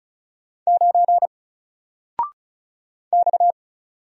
Morse code, heard as a keyed tone, about 36 wpm, 700 Hz. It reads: 9 X
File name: Single-Letter-Number-35wpm